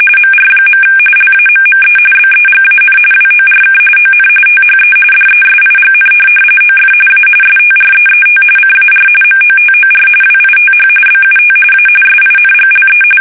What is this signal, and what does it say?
RTTY, about 45.45 baud. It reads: MR. LAMME BEGAN WORK ON THE DESIGN OF A STREET CAR MOTOR WITH BUT A SINGLE
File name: fsk850 / benjamin-g-lamme